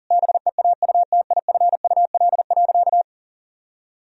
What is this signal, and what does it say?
Morse code, heard as a keyed tone, about 40 wpm, 700 Hz. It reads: BEAUTIFUL.
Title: Sentences from Top 1000 Words - Review Part 2 - 40wpm